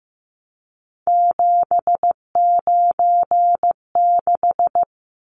Morse code, heard as a keyed tone, about 15 wpm, 700 Hz. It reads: 796
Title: Numbers - 3-Digit 15wpm